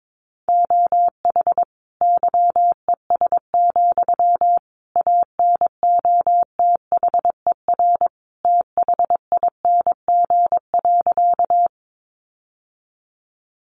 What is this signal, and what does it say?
Morse code, heard as a keyed tone, about 22 wpm, 700 Hz. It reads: OH YES, ANOTHER THING.
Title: Sentences from Top 600 Words 22wpm